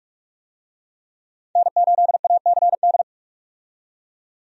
Morse code, heard as a keyed tone, about 45 wpm, 700 Hz. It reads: N8ACD
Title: USA Call Signs - 5 Characters - 45wpm